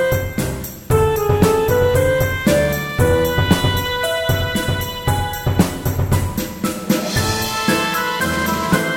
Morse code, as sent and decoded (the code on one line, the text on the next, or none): none